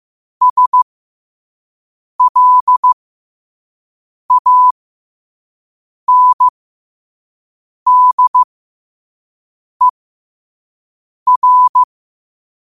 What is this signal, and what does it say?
Morse code, heard as a keyed tone, about 15 wpm, spaced as for 5 wpm, 1000 Hz. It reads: SLANDER
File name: qotdfiles / 5wpm